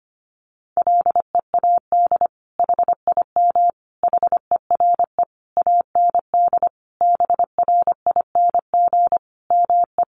Morse code, heard as a keyed tone, about 25 wpm, 700 Hz. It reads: LEAD HIM HERE AND BRING ME
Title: Sentences from Top 1200 Words 25wpm